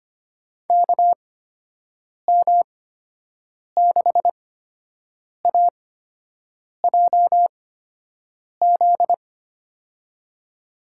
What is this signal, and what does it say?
Morse code, heard as a keyed tone, about 25 wpm, spaced as for 7 wpm, 700 Hz. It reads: KM6AJZ
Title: USA Call Signs - 6 Characters - 8x Character Spacing - 25wpm